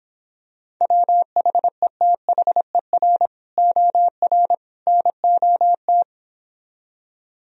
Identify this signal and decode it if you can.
Morse code, heard as a keyed tone, about 26 wpm, 700 Hz. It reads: WHETHER OR NOT